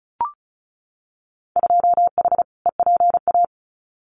Morse code, heard as a keyed tone, about 34 wpm, 700 Hz. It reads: 2H EPA
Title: Winter Field Day Exchanges Encore 35wpm